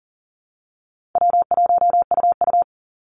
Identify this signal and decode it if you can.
Morse code, heard as a keyed tone, about 38 wpm, 700 Hz. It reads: W1UU
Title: Easy Call Signs - 40wpm